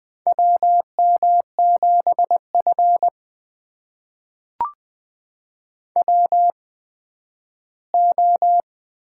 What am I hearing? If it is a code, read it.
Morse code, heard as a keyed tone, about 20 wpm, spaced as for 12 wpm, 700 Hz. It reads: WM7F WO
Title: USA Call Signs - 4 Characters - 8x Character Spacing - 20wpm